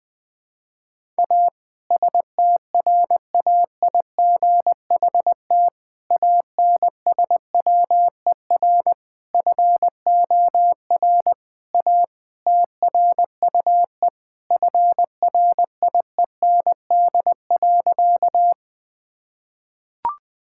Morse code, heard as a keyed tone, about 20 wpm, 700 Hz. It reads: A STRAIGHT ANSWER FOR A TRUE FRIEND.
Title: Sentences from Top 1000 Words - Review Part 6 - 20wpm